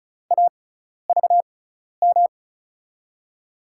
Morse code, heard as a keyed tone, about 36 wpm, spaced as for 12 wpm, 700 Hz. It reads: AVM
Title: USA Call Signs - 5 Characters - 6x Character Spacing - 35wpm